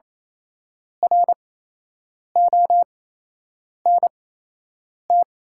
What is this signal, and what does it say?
Morse code, heard as a keyed tone, about 28 wpm, spaced as for 8 wpm, 700 Hz. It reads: RONT
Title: 8x Character Spacing - Top 500 Words - 28wpm